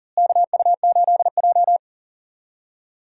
Morse code, heard as a keyed tone, about 37 wpm, 700 Hz. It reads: KU8J
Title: USA Call Signs - 4 Characters - Repeated 2x - 40wpm